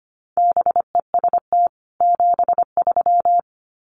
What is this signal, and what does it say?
Morse code, heard as a keyed tone, about 25 wpm, 700 Hz. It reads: BEST 73
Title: Common QSO Phrases 25wpm